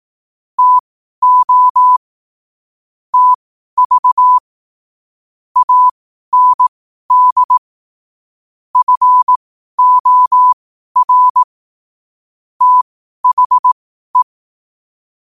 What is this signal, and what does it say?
Morse code, heard as a keyed tone, about 18 wpm, spaced as for 12 wpm, 1000 Hz. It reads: TO TV AND FOR THE
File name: qotdfiles / 12wpm